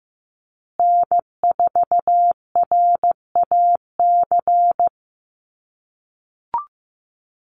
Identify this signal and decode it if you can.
Morse code, heard as a keyed tone, about 15 wpm, 700 Hz. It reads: N4RAC
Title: USA Call Signs Repeated 2x 15wpm